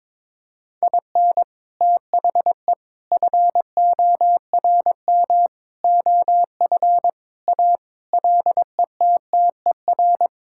Morse code, heard as a keyed tone, about 22 wpm, 700 Hz. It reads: IN THE FORM OF A LETTER
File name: practice / 22wpm / Sentences with 4-Word Phrases - 22wpm